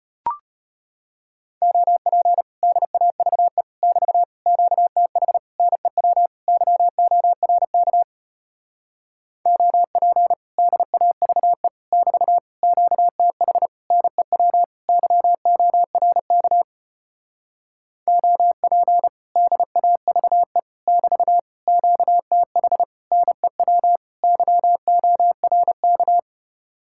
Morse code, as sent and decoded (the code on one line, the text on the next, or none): --- .--. / -.. .- ...- . / -...- / --.- - .... / -. .-- / -.-- --- .-. -.- / --- .--. / -.. .- ...- . / -...- / --.- - .... / -. .-- / -.-- --- .-. -.- / --- .--. / -.. .- ...- . / -...- / --.- - .... / -. .-- / -.-- --- .-. -.-
OP DAVE <BT> QTH NEW YORK OP DAVE <BT> QTH NEW YORK OP DAVE <BT> QTH NEW YORK